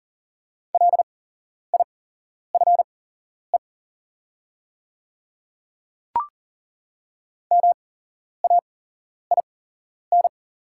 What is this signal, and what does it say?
Morse code, heard as a keyed tone, about 39 wpm, spaced as for 11 wpm, 700 Hz. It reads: LIFE MAIN
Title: 8x Character Spacing - Top 300 Words - 40wpm